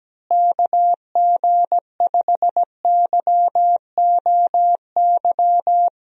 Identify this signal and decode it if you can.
Morse code, heard as a keyed tone, about 17 wpm, 700 Hz. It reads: KG5YOY
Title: USA Call Signs - 6 Characters - Repeated 3x - 17wpm